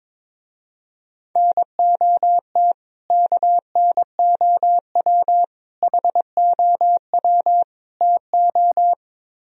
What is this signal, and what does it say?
Morse code, heard as a keyed tone, about 22 wpm, 700 Hz. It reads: NOT KNOW HOW TO
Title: Speed-Racing - Sets of 4 Words 22wpm